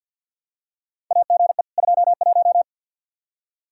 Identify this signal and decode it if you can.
Morse code, heard as a keyed tone, about 52 wpm, 700 Hz. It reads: AGE 21
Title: Practice QSO - Exchange 3 - More Info - 50wpm